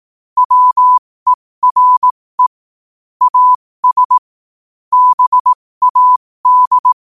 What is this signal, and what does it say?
Morse code, heard as a keyed tone, about 18 wpm, spaced as for 15 wpm, 1000 Hz. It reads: WERE AS BAD
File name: qotdfiles / 15wpm